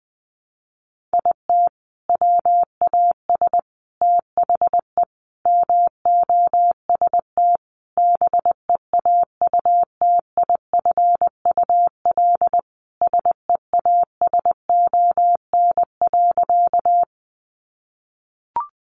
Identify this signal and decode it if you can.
Morse code, heard as a keyed tone, about 19 wpm, 700 Hz. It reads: IT WAS THE MOST BEAUTIFUL SEASON.
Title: Sentences from Top 1400 Words 20wpm